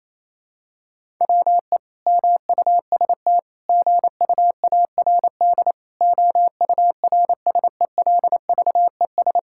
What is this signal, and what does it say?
Morse code, heard as a keyed tone, about 27 wpm, 700 Hz. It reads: WE MUST GUARD OURSELVES
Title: Sentences from Top 1600 Words 28wpm